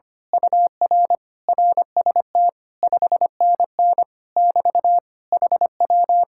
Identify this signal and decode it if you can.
Morse code, heard as a keyed tone, about 25 wpm, 700 Hz. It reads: UR RST 5NN <BT> HW
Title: Speed-Racing Common QSO Phrases - 25wpm